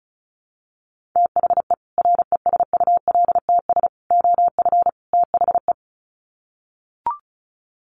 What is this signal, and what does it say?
Morse code, heard as a keyed tone, about 34 wpm, 700 Hz. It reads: THE RESULTS OF THE
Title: Sets of 4 Words 35wpm